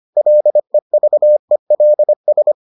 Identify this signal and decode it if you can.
Morse code, heard as a keyed tone, about 25 wpm, 600 Hz. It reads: LEVELS